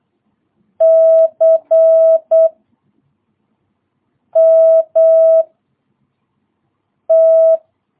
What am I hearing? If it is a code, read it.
Morse code, heard as a keyed tone, about 8 wpm, 650 Hz. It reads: C M T
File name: jh8cbh / 2022cw